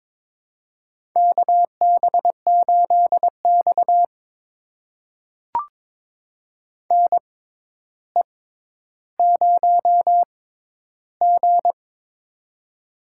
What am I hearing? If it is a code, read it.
Morse code, heard as a keyed tone, about 23 wpm, spaced as for 11 wpm, 700 Hz. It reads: KB8X NE0G
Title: USA Call Signs - 4 Characters - 6x Character Spacing - 22wpm